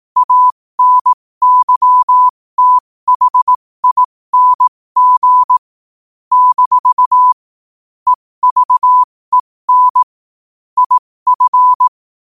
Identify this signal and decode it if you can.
Morse code, heard as a keyed tone, about 18 wpm, spaced as for 15 wpm, 1000 Hz. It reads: ANYTHING - EVEN IF